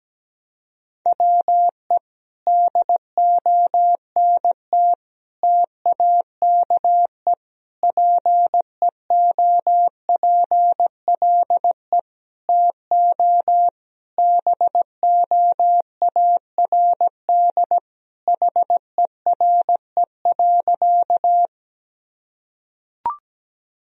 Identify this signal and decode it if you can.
Morse code, heard as a keyed tone, about 17 wpm, 700 Hz. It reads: WE DONT TAKE PEOPLE TO BOARD HERE.
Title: Sentences from Top 1000 Words - Review Part 4 - 17wpm